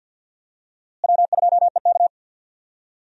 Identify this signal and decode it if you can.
Morse code, heard as a keyed tone, about 47 wpm, 700 Hz. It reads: W2EK
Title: USA Call Signs - 4 Characters - Repeated 2x - 50wpm